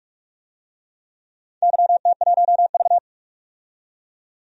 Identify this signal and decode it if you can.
Morse code, heard as a keyed tone, about 46 wpm, 700 Hz. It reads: YT1V
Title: DXCC-100 Call Signs - 4 Characters - 45wpm